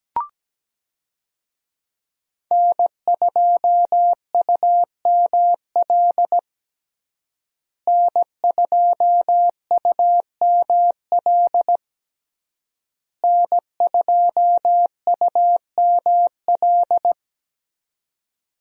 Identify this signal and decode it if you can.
Morse code, heard as a keyed tone, about 17 wpm, 700 Hz. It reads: N2UML N2UML N2UML
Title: USA Call Signs - 5 Characters - Repeated 3x - 17wpm